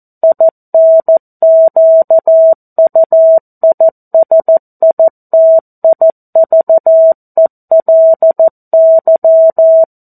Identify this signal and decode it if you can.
Morse code, heard as a keyed tone, about 14 wpm, 650 Hz. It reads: INQUISITIVELY
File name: 02-Chapter-02 14 WPM